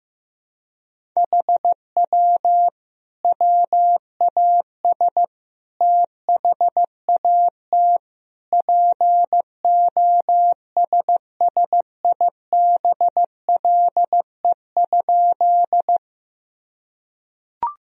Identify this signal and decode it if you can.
Morse code, heard as a keyed tone, about 15 wpm, 700 Hz. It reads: HW WAS THAT POSSIBLE?
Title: Sentences from Top 500 Words and Abbreviations 15wpm